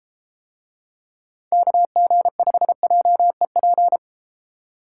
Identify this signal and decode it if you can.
Morse code, heard as a keyed tone, about 33 wpm, 700 Hz. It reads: KG5JEP